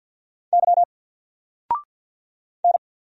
Morse code, heard as a keyed tone, about 50 wpm, 700 Hz. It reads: Y N
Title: Single Letter-Number 50wpm